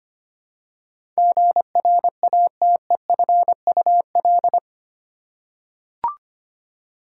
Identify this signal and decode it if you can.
Morse code, heard as a keyed tone, about 25 wpm, 700 Hz. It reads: GRATEFUL
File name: 8 Letter Words - First 3 Letters Given - 25wpm